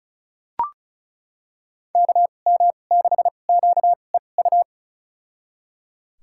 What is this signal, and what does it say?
Morse code, heard as a keyed tone, about 34 wpm, spaced as for 25 wpm, 700 Hz. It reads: KM6QEU
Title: USA Call Signs - 6 Characters - 2x Character Spacing - 35wpm